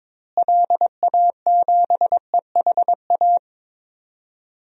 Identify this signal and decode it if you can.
Morse code, heard as a keyed tone, about 22 wpm, 700 Hz. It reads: LA7EHA
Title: DXCC-100 Call Signs - 6 Characters - Repeated 3x - 22wpm